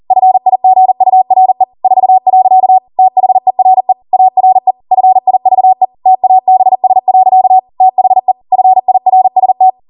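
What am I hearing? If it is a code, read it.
Morse code, heard as a keyed tone, about 40 wpm, 750 Hz. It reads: FIGURE 4. THERE ARE FIVE TABS. THE FIRST
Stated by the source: Code practice at 40 wpm.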